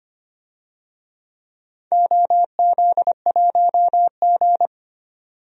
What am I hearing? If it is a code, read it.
Morse code, heard as a keyed tone, about 25 wpm, 700 Hz. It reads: OZ1G
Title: DXCC-100 Call Signs - 4 Characters - 25wpm